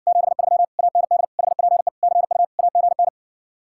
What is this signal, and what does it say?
Morse code, heard as a keyed tone, about 60 wpm, 700 Hz. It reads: 73 AND HPE CU AGN